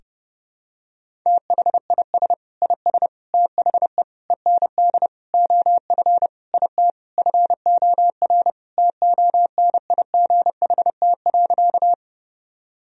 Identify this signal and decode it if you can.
Morse code, heard as a keyed tone, about 30 wpm, 700 Hz. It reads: THIS IS THE END OF IT FOR TONIGHT.
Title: Sentences from Top 1000 Words - Review Part 3 - 30wpm